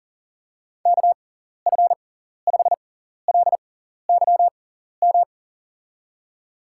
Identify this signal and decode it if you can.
Morse code, heard as a keyed tone, about 40 wpm, spaced as for 14 wpm, 700 Hz. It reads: KF5LYM
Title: USA Call Signs - 6 Characters - 6x Character Spacing - 40wpm